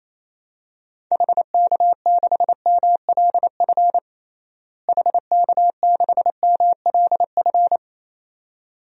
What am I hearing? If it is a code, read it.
Morse code, heard as a keyed tone, about 28 wpm, 700 Hz. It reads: HK6MLF HK6MLF
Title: DXCC-100 Call Signs - 6 Characters - Repeated 2x - 28wpm